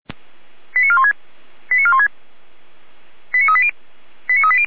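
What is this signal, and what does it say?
Five-tone selective call (ZVEI), heard as quick runs of five tones, tones of about 70 ms. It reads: 89416 89416 89380 89380
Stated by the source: Received on SDR as ZVEI five-tone selective call.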